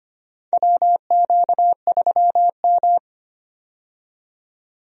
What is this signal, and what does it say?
Morse code, heard as a keyed tone, about 25 wpm, 700 Hz. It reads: WQ3M
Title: USA Call Signs - 4 Characters - 25wpm